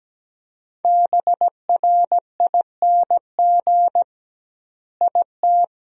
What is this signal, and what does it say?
Morse code, heard as a keyed tone, about 17 wpm, 700 Hz. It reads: BRING IT
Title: 2x Word-Spacing - Sentences from Top 500 Words - 17wpm